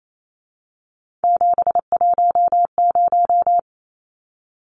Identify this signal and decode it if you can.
Morse code, heard as a keyed tone, about 28 wpm, 700 Hz. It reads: 710
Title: Numbers - 3-Digit 28wpm